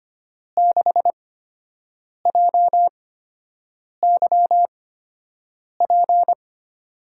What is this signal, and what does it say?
Morse code, heard as a keyed tone, about 25 wpm, spaced as for 7 wpm, 700 Hz. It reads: 6JYP
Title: USA Call Signs - 8x Character Spacing - 25wpm